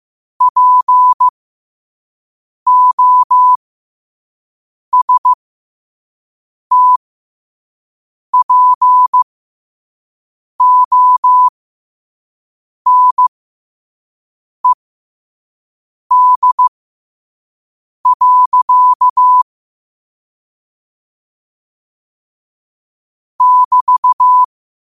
Morse code, heard as a keyed tone, about 15 wpm, spaced as for 5 wpm, 1000 Hz. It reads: POSTPONED. <BT>
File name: qotdfiles / 5wpm